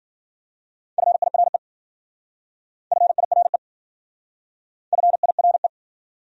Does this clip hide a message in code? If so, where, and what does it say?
Yes: Morse code from 1.0 to 5.7 s, heard as a keyed tone, about 52 wpm, 700 Hz. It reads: FIRE FIRE FIRE